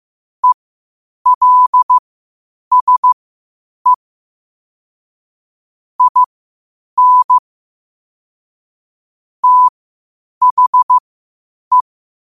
Morse code, heard as a keyed tone, about 15 wpm, spaced as for 8 wpm, 1000 Hz. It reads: ELSE IN THE
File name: qotdfiles / 8wpm